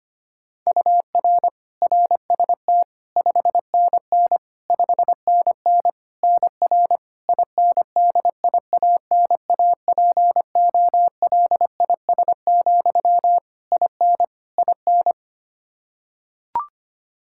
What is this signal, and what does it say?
Morse code, heard as a keyed tone, about 25 wpm, 700 Hz. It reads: UR RST 5NN 5NN NR INDIANAPOLIS, IN IN
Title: K1USN SST QSO Practice 25wpm